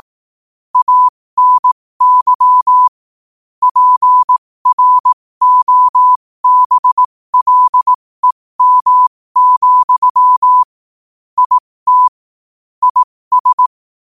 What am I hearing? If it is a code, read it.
Morse code, heard as a keyed tone, about 18 wpm, spaced as for 15 wpm, 1000 Hz. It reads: ANY PROBLEM, IT IS
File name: qotdfiles / 15wpm